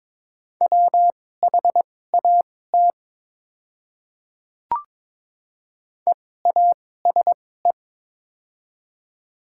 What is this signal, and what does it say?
Morse code, heard as a keyed tone, about 22 wpm, spaced as for 16 wpm, 690 Hz. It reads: WHAT EASE
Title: 2x Character Spacing - Top 300 Words - 22wpm